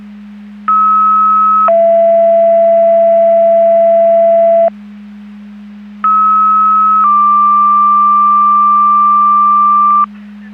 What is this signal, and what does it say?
Two-tone page, heard as a one-second tone, then a three-second tone, about 1290 then 670 Hz, about 1290 then 1180 Hz. It is heard from 0.7 to 10.0 s.
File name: Portage 154.385 Paging